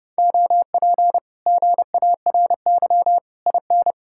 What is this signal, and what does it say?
Morse code, heard as a keyed tone, about 31 wpm, 700 Hz. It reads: OP GARY IN